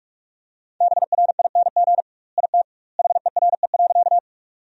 Morse code, heard as a keyed tone, about 47 wpm, 700 Hz. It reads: BRING IT HERE.
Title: 2x Word-Spacing - Sentences from Top 500 Words - 45wpm